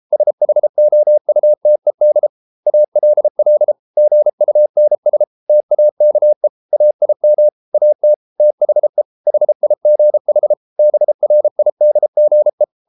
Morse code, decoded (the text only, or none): SHOUTED ALL GUNS TAKE AIM AT THE HIGH BRIDGE